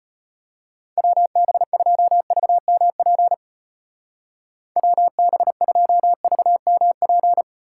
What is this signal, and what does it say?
Morse code, heard as a keyed tone, about 30 wpm, 700 Hz. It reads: WB2VMP WB2VMP
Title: Speed Racing - USA Call Signs 25wpm